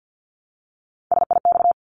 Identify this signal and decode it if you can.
Morse code, heard as a keyed tone, about 50 wpm, 700 Hz. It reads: SIX
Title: Top 500 Words 50wpm